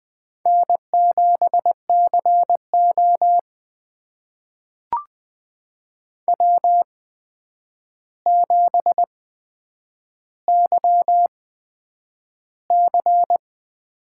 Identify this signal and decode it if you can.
Morse code, heard as a keyed tone, about 20 wpm, spaced as for 9 wpm, 700 Hz. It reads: N7CO W7YC
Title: USA Call Signs - 4 Characters - 8x Character Spacing - 20wpm